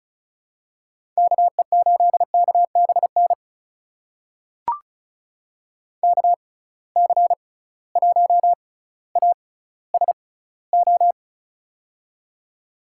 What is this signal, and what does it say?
Morse code, heard as a keyed tone, about 36 wpm, spaced as for 18 wpm, 700 Hz. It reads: KE8KBN KC1ASO
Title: USA Call Signs - 6x Character Spacing - 35wpm